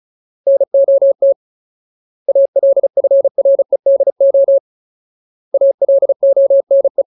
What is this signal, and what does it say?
Morse code, heard as a keyed tone, about 34 wpm, 550 Hz. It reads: NOT ALFREDO ALONE
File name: Ultimate Word List / 35WPM Ultimate